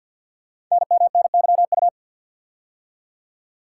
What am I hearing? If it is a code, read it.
Morse code, heard as a keyed tone, about 48 wpm, 690 Hz. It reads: NMNYU
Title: Code Groups - 5 Letters - 50wpm